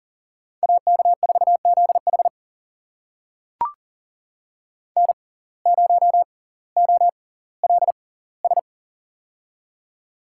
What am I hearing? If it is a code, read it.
Morse code, heard as a keyed tone, about 37 wpm, spaced as for 20 wpm, 700 Hz. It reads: AK4ZH N0OLS
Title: USA Call Signs - 5 Characters - 6x Character Spacing - 40wpm